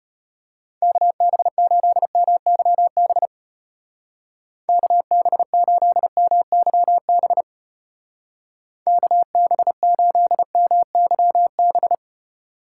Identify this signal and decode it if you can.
Morse code, heard as a keyed tone, about 32 wpm, 700 Hz. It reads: KB8MYB KB8MYB KB8MYB